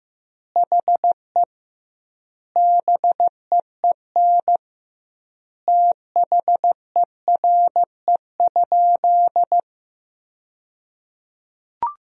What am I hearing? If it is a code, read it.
Morse code, heard as a keyed tone, about 15 wpm, 700 Hz. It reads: HE BEEN THERE?
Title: 2x Word-Spacing - Sentences from Top 100 Words - 15wpm